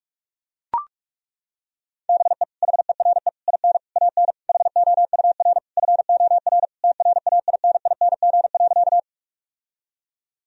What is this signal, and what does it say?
Morse code, heard as a keyed tone, about 42 wpm, 700 Hz. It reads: BE HERE IN AN HOUR FOR TRAINING.
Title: Sentences from Top 1000 Words - Review Part 4 - 45wpm